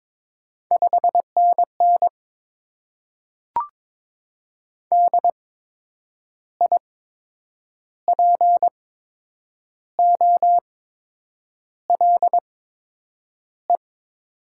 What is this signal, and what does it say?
Morse code, heard as a keyed tone, about 23 wpm, spaced as for 8 wpm, 700 Hz. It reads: 5NN DIPOLE
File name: Top 90 QSO Elements - 8x Character Spacing - 22wpm